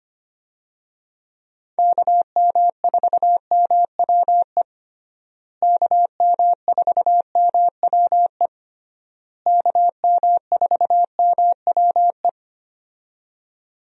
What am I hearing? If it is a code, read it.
Morse code, heard as a keyed tone, about 25 wpm, 700 Hz. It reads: KM4MWE KM4MWE KM4MWE
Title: USA Call Signs - 6 Characters - Repeated 3x - 25wpm